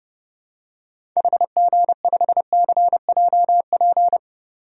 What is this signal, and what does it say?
Morse code, heard as a keyed tone, about 32 wpm, 700 Hz. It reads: HG5CJP
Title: DXCC-100 Call Signs - 6 Characters - 30wpm